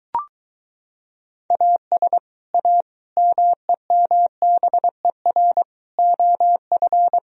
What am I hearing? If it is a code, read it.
Morse code, heard as a keyed tone, about 24 wpm, 700 Hz. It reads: AS A MEMBER OF